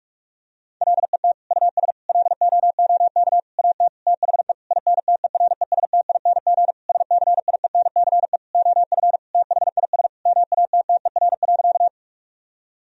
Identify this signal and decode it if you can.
Morse code, heard as a keyed tone, about 47 wpm, 700 Hz. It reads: LET US LOOK AT THE INTERESTING SCIENCE OF THIS MATTER.
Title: Sentences from Top 1000 Words - Review Part 2 - 45wpm